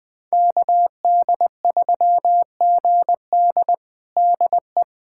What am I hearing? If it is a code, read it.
Morse code, heard as a keyed tone, about 20 wpm, 700 Hz. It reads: KD3GD DE
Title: USA Callsign DE Callsign - 5 Characters - 20wpm